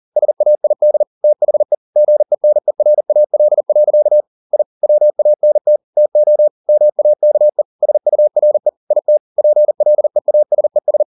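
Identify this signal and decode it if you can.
Morse code, heard as a keyed tone, about 40 wpm, 600 Hz. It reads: SAID THE GENERAL. I WANT TO MAKE SURE IT PLEASES
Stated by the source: Morse code at 40 wpm, practice text, CW med Gnister.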